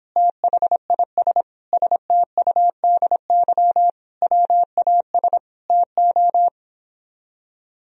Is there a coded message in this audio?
Yes: Morse code from 0.2 to 6.5 s, heard as a keyed tone, about 29 wpm, 700 Hz. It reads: THIS STUDY WAS TO